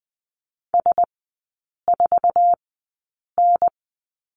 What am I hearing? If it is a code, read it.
Morse code, heard as a keyed tone, about 20 wpm, spaced as for 8 wpm, 700 Hz. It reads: S4N
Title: 060 Lesson 11e - Letter h - Three Characters 20wpm